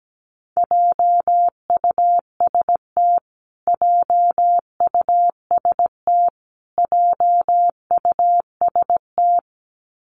Morse code, heard as a keyed tone, about 17 wpm, 700 Hz. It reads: JUST JUST JUST
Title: Top 500 Words Repeated 17wpm